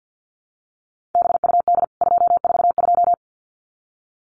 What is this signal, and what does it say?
Morse code, heard as a keyed tone, about 50 wpm, 700 Hz. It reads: BUD 242
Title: CWT Exchange 50wpm